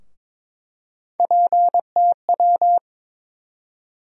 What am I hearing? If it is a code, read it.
Morse code, heard as a keyed tone, about 22 wpm, 700 Hz. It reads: PTW
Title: USA Call Sign Postfixes - 3 Letters 22wpm